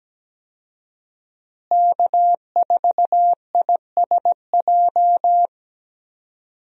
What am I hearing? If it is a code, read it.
Morse code, heard as a keyed tone, about 17 wpm, 700 Hz. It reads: K4ISJ
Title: USA Call Signs - 5 Characters - Repeated 2x - 17wpm